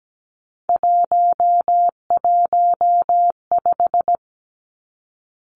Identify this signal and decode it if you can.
Morse code, heard as a keyed tone, about 17 wpm, 700 Hz. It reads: 115
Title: Three Challenging Letter-Number Combinations 17wpm